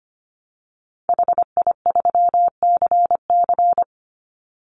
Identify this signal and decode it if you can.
Morse code, heard as a keyed tone, about 25 wpm, 700 Hz. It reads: HI3CC
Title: Call Signs - 25wpm